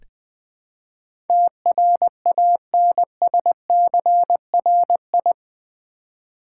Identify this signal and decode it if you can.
Morse code, heard as a keyed tone, about 20 wpm, 700 Hz. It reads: TRANSCRI
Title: Build A Word - 4 Letters At A Time - 10-Letter Words - 20wpm